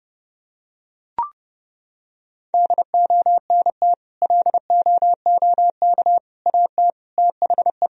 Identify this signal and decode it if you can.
Morse code, heard as a keyed tone, about 32 wpm, 700 Hz. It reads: DONT LOOK AT THE
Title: Sentences from Top 1000 Words - Review Part 1 - 30wpm